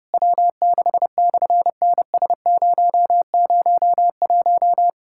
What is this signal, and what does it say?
Morse code, heard as a keyed tone, about 28 wpm, 700 Hz. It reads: W6/NS001
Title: SOTA QSO Practice 30wpm